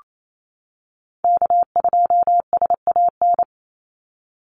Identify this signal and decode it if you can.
Morse code, heard as a keyed tone, about 28 wpm, 700 Hz. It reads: K2SAN
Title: Call Signs - 28wpm